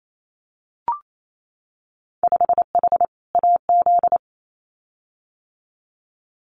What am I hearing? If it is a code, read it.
Morse code, heard as a keyed tone, about 28 wpm, 700 Hz. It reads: 5H AZ
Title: Winter Field Day Exchanges Encore 28wpm